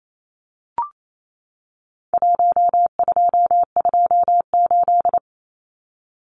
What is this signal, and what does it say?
Morse code, heard as a keyed tone, about 28 wpm, 700 Hz. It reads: 1228
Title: Numbers - Contest Style - 4-Digit 28wpm